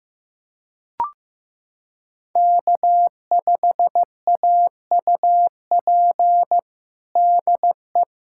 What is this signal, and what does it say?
Morse code, heard as a keyed tone, about 15 wpm, 700 Hz. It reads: K5AUP DE
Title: USA Callsign DE Callsign 15wpm